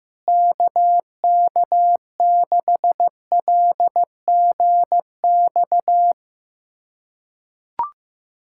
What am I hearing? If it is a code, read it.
Morse code, heard as a keyed tone, about 15 wpm, 700 Hz. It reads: KK6LGX
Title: USA Callsign DE Callsign - 6 Characters - 15wpm